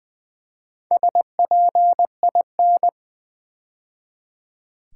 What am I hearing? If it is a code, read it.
Morse code, heard as a keyed tone, about 20 wpm, 700 Hz. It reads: SPIN
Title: Build A Word - 2 Letters At A Time - 4-Letter Words - 20wpm